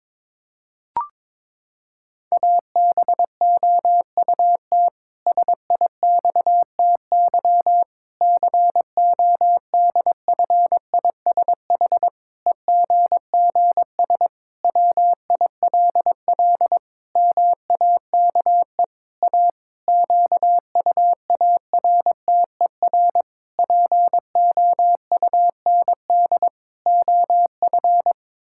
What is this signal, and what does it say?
Morse code, heard as a keyed tone, about 22 wpm, 700 Hz. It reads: ABOUT SIXTY CODFISH EGGS WILL MAKE A QUARTER POUND OF
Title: Pangram Challenge v2 - 22wpm